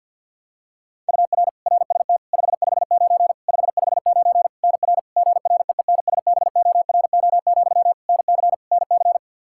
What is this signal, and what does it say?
Morse code, heard as a keyed tone, about 53 wpm, 700 Hz. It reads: UR RST 559 559 NR GREENSBORO, NC NC
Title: K1USN SST QSO Practice 50wpm